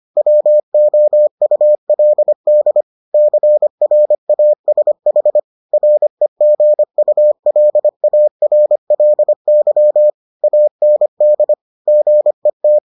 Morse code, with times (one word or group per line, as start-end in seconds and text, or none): 0.2-2.8 WOULD
3.1-5.4 CRASH
5.7-10.1 REGULARLY
10.4-11.5 AND
11.9-12.8 GET